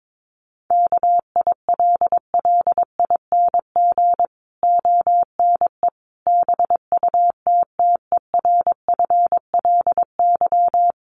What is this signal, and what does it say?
Morse code, heard as a keyed tone, about 22 wpm, 700 Hz. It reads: KILLING ONE BUTTERFLY
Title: A Sound Of Thunder 22wpm